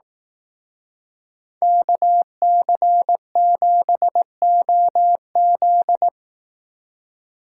Morse code, heard as a keyed tone, about 18 wpm, 700 Hz. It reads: KC7OZ
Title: Speed Racing - USA 2x2 Call Signs 15wpm